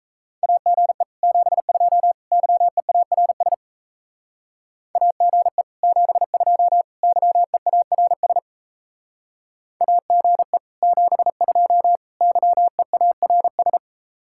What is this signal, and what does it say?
Morse code, heard as a keyed tone, about 34 wpm, 700 Hz. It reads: AGE 72 YEARS AGE 72 YEARS AGE 72 YEARS
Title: Speed-Racing Common QSO Phrases - 28wpm